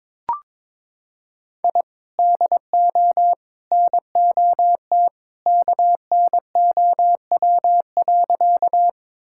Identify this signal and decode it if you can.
Morse code, heard as a keyed tone, about 22 wpm, 700 Hz. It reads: I DO NOT KNOW.
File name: practice / 22wpm / Sentences with 3-Word Phrases - 22wpm